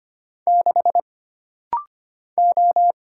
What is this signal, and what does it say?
Morse code, heard as a keyed tone, about 25 wpm, 700 Hz. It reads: 6 O
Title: Single Letter-Number Alt - Rapid-Fire 25wpm